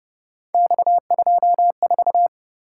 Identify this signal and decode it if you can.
Morse code, heard as a keyed tone, about 30 wpm, 700 Hz. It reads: X24